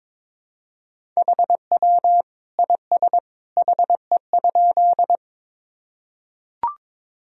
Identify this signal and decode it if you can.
Morse code, heard as a keyed tone, about 22 wpm, 700 Hz. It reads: HW IS HE?
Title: Sentences from Top 100 Words and Abbreviations 22wpm